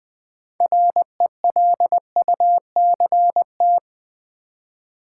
Morse code, heard as a keyed tone, about 20 wpm, 700 Hz. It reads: RELUCT
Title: Build A Word - 3 Letters At A Time - 9-Letter Words - 20wpm